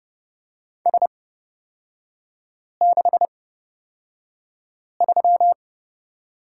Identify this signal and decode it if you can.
Morse code, heard as a keyed tone, about 30 wpm, spaced as for 5 wpm, 700 Hz. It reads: S63
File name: Lesson 36e - BT - Three Characters 30wpm